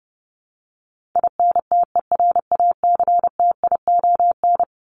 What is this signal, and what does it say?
Morse code, heard as a keyed tone, about 30 wpm, 700 Hz. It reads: INTERACTION
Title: Top 1251-1500 Words 30wpm